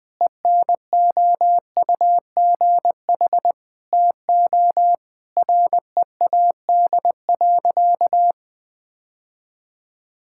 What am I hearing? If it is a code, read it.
Morse code, heard as a keyed tone, about 20 wpm, 700 Hz. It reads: ENOUGH TO READ.
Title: Sentences from Top 1000 Words - Review Part 5 - 20wpm